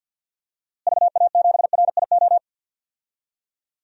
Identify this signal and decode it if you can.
Morse code, heard as a keyed tone, about 52 wpm, 700 Hz. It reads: VA7RIO
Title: DXCC-100 Call Signs - 6 Characters - Repeated 2x - 50wpm